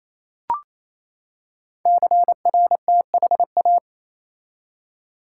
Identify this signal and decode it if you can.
Morse code, heard as a keyed tone, about 28 wpm, 700 Hz. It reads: CRTHA
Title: Code Groups - 5 Letters - 28wpm